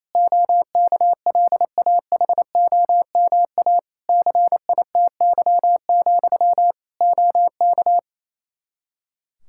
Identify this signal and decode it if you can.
Morse code, heard as a keyed tone, about 28 wpm, 700 Hz. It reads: OKLAHOMA CITY, OK
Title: Practice QSO - Exchange 1 - Beginning - 28wpm